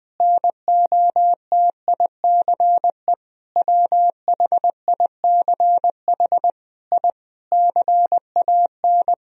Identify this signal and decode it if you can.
Morse code, heard as a keyed tone, about 21 wpm, 700 Hz. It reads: NOTICE WHICH I CAN
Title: Sentences from Top 1000 Words - Review Part 2 - 20wpm